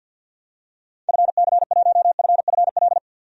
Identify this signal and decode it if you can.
Morse code, heard as a keyed tone, about 46 wpm, 700 Hz. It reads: FC1FFL
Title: DXCC-100 Call Signs - 6 Characters - 50wpm